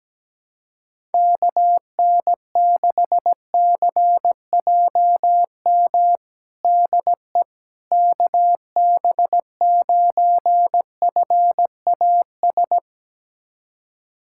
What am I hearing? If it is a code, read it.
Morse code, heard as a keyed tone, about 17 wpm, 700 Hz. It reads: KN6CJM DE KB9FAS
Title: USA Callsign DE Callsign 17wpm